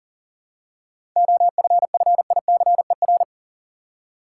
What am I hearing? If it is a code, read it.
Morse code, heard as a keyed tone, about 40 wpm, 700 Hz. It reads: OFFICER
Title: Seven Letter Words - 40wpm